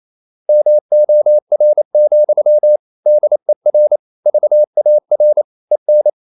Morse code, heard as a keyed tone, about 28 wpm, 600 Hz. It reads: MOR, DER VAR EN
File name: Ferie QRQ 28wpm